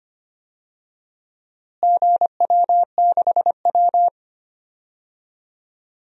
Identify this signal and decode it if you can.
Morse code, heard as a keyed tone, about 25 wpm, 700 Hz. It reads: GW6W
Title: DXCC-100 Call Signs - 4 Characters - Repeated 2x - 25wpm